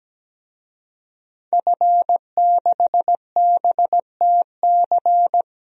Morse code, heard as a keyed tone, about 17 wpm, 700 Hz. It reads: F6BTC
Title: DXCC-100 Call Signs - 5 Characters - Repeated 2x - 17wpm